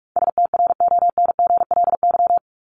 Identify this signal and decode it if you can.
Morse code, heard as a keyed tone, about 44 wpm, 700 Hz. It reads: STRONGLY
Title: LY Suffix - Top 100 Words with Base Words 45wpm